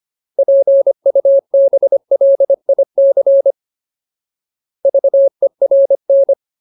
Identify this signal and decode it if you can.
Morse code, heard as a keyed tone, about 25 wpm, 550 Hz. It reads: PUBLIC VERN
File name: Ultimate Word List / 25WPM Ultimate